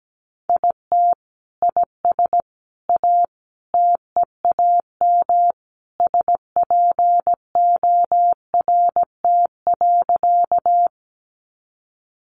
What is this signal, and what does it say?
Morse code, heard as a keyed tone, about 17 wpm, 700 Hz. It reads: IT IS A TEAM SPORT.
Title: Sentences from Top 1500 Words 17wpm